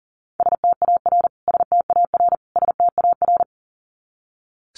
Morse code, heard as a keyed tone, about 37 wpm, 700 Hz. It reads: STAR STAR STAR
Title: Top 500 Words Repeated 40wpm